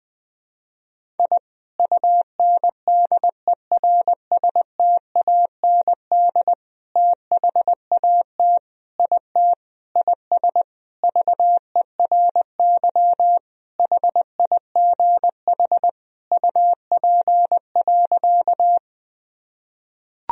Morse code, heard as a keyed tone, about 20 wpm, 700 Hz. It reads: I UNDERSTAND THAT IT IS VERY HIGH UP.
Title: Sentences from Top 1000 Words - Review Part 1 - 20wpm